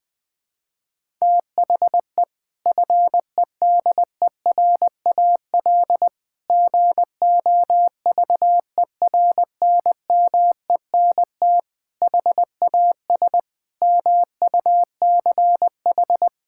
Morse code, heard as a keyed tone, about 21 wpm, 700 Hz. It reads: THE FEDERAL GOVERNMENT HAS MUCH
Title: Sentences from Top 1000 Words - Review Part 5 - 20wpm